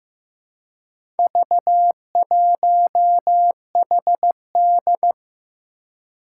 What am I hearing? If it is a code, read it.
Morse code, heard as a keyed tone, about 15 wpm, 700 Hz. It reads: V1HD
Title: Four Challenging Letter-Number Combinations 15wpm